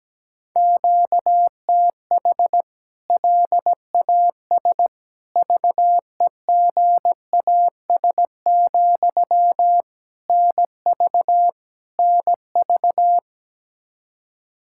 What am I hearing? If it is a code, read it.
Morse code, heard as a keyed tone, about 17 wpm, 700 Hz. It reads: QTH LAS VEGAS, NV NV
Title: K1USN SST QSO Practice 17wpm